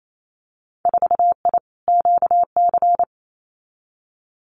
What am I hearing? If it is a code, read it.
Morse code, heard as a keyed tone, about 28 wpm, 700 Hz. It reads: 4I QC
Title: Winter Field Day Exchanges Encore 28wpm